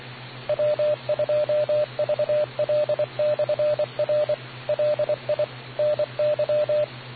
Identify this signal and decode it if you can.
Morse code, heard as a keyed tone, about 24 wpm, 600 Hz, with radio noise. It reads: W2VL/R LI NY